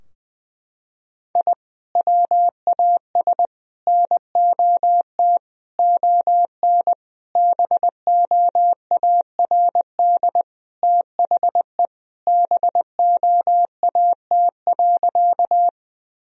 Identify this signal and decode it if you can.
Morse code, heard as a keyed tone, about 20 wpm, 700 Hz. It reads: I WAS NOT ON BOARD THE BOAT.
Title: Sentences from Top 1000 Words - Review Part 4 - 20wpm